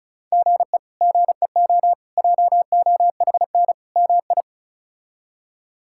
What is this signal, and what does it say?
Morse code, heard as a keyed tone, about 34 wpm, 700 Hz. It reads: GE GEO JOHN MI
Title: K1USN SST QSO Practice - 35wpm